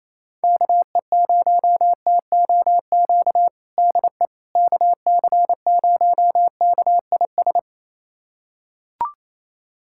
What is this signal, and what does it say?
Morse code, heard as a keyed tone, about 28 wpm, 700 Hz. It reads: KE0TOQ DE KC0KIS
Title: USA Callsign DE Callsign - 6 Characters - 28wpm